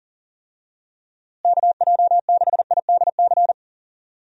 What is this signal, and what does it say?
Morse code, heard as a keyed tone, about 40 wpm, 700 Hz. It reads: KJ6IDC
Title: USA Call Signs - 6 Characters - Repeated 2x - 40wpm